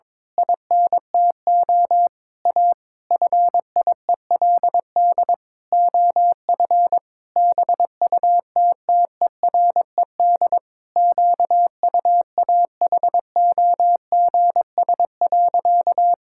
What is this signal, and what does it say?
Morse code, heard as a keyed tone, about 22 wpm, 700 Hz. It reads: INTO A FIELD OF BUTTERED QUAHOGS.
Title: Pangram Challenge v2 - 22wpm